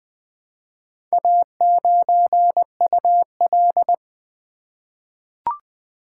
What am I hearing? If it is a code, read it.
Morse code, heard as a keyed tone, about 20 wpm, 700 Hz. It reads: A9UL
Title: Four Letter-Number Combinations - 20wpm